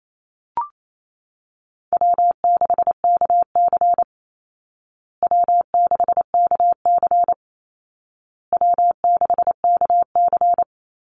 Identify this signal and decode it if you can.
Morse code, heard as a keyed tone, about 28 wpm, 700 Hz. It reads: W6KC W6KC W6KC
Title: Easy Call Signs Repeated 3x - 28wpm